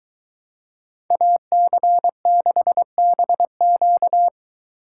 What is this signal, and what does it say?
Morse code, heard as a keyed tone, about 23 wpm, 700 Hz. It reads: AC6BQ